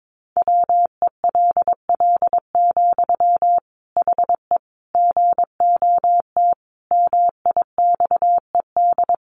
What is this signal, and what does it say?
Morse code, heard as a keyed tone, about 22 wpm, 700 Hz. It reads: WELL, HE GOT MIXED
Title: Sentences from Top 1500 Words 22wpm